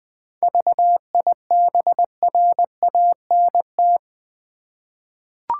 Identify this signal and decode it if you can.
Morse code, heard as a keyed tone, about 20 wpm, 700 Hz. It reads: VIBRANT
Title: 7 Letter Words - First 3 Letters Given - 20wpm